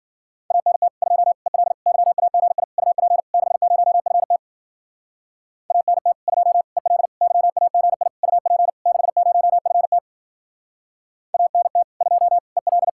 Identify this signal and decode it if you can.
Morse code, heard as a keyed tone, about 51 wpm, 700 Hz. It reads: ANT 2 EL YAGI UP 60FT ANT 2 EL YAGI UP 60FT ANT 2 EL